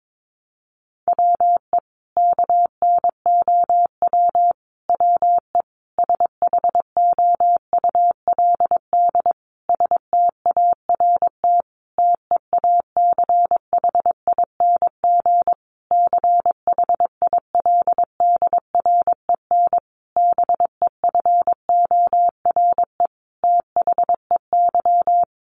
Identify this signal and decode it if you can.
Morse code, heard as a keyed tone, about 22 wpm, 700 Hz. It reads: WE KNOW WE SHOULD START TEACHING CHILDREN BEFORE THEY
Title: Sentences from Top 1100 Words 22wpm